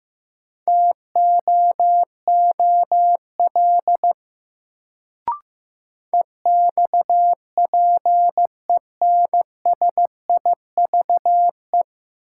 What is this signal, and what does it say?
Morse code, heard as a keyed tone, about 15 wpm, 700 Hz. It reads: TOOL EXPENSIVE
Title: Sentences from Top 1000 Words - Review Part 6 - New Word Drill - 15wpm